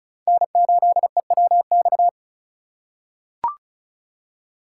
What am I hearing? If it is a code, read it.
Morse code, heard as a keyed tone, about 36 wpm, 700 Hz. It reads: N8EWX
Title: K1USN SST QSO Practice - 35wpm